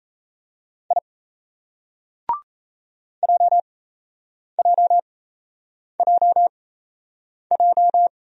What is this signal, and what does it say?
Morse code, heard as a keyed tone, about 37 wpm, spaced as for 8 wpm, 700 Hz. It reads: I JJJJ